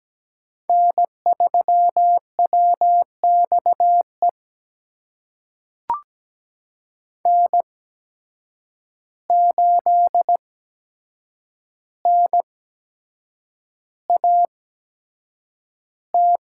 Morse code, heard as a keyed tone, about 17 wpm, spaced as for 7 wpm, 700 Hz. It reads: N3WXE N8NAT
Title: USA Call Signs - 8x Character Spacing - 17wpm